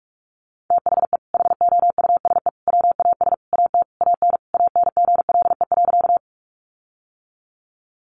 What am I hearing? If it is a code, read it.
Morse code, heard as a keyed tone, about 44 wpm, 700 Hz. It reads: THE HOUSE WAS AT AN ANGLE.
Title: Sentences from Top 1500 Words 45wpm